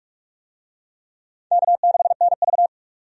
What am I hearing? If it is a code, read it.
Morse code, heard as a keyed tone, about 44 wpm, 700 Hz. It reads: K6NV
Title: USA Call Signs - 4 Characters - 45wpm